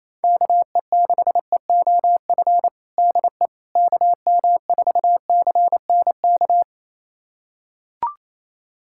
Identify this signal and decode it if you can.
Morse code, heard as a keyed tone, about 27 wpm, 700 Hz. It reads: KE6EOF DE KM4CNK
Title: USA Callsign DE Callsign - 6 Characters - 28wpm